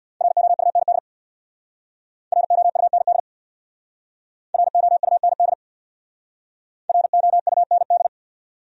Morse code, heard as a keyed tone, about 64 wpm, 700 Hz. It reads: ROUND ROUND ROUND ROUND